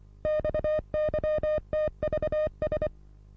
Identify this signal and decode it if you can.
Morse code, heard as a keyed tone, about 25 wpm, 610 Hz. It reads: XYTVS
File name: repos / Morsekurs / DJ4UF Morsen27